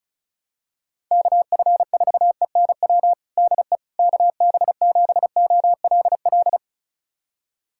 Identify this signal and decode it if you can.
Morse code, heard as a keyed tone, about 36 wpm, 700 Hz. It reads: KF4ENW DE KB7OLL
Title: USA Callsign DE Callsign - 6 Characters - 35wpm